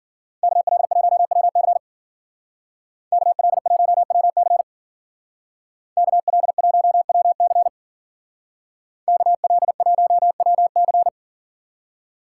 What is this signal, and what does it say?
Morse code, heard as a keyed tone, about 49 wpm, 700 Hz. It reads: KL1WC KL1WC KL1WC KL1WC
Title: Speed Racing - USA 2x2 Call Signs 40wpm